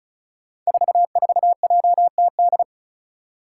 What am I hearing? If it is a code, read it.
Morse code, heard as a keyed tone, about 36 wpm, 700 Hz. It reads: 44JTD
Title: Five Challenging Letter-Number Combinations 35wpm